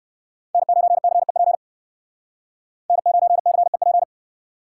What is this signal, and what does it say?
Morse code, heard as a keyed tone, about 57 wpm, 700 Hz. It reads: N9CEP N9CEP